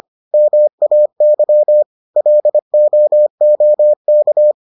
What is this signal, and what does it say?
Morse code, heard as a keyed tone, about 25 wpm, 600 Hz. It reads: MAY LOOK